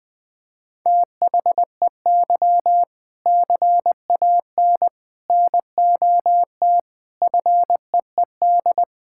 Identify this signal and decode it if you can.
Morse code, heard as a keyed tone, about 20 wpm, 700 Hz. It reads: THEY CAN NOT FEED
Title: Sentences from Top 1600 Words 20wpm